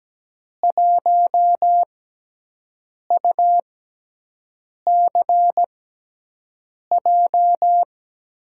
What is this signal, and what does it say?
Morse code, heard as a keyed tone, about 17 wpm, spaced as for 6 wpm, 700 Hz. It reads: 1UCJ
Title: USA Call Signs - 5 Characters - 6x Character Spacing - 17wpm